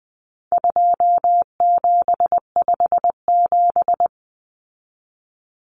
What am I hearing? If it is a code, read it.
Morse code, heard as a keyed tone, about 20 wpm, 700 Hz. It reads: 2757
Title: Common Cut Numbers - 4-Digit 20wpm